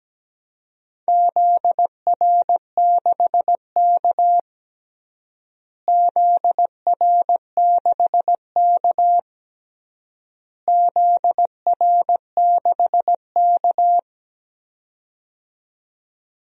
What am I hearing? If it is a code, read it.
Morse code, heard as a keyed tone, about 17 wpm, 700 Hz. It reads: ZR6K ZR6K ZR6K
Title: DXCC-100 Call Signs - 4 Characters - Repeated 3x - 17wpm